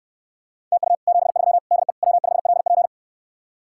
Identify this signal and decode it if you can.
Morse code, heard as a keyed tone, about 69 wpm, 700 Hz. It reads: TU 73 DE W5LP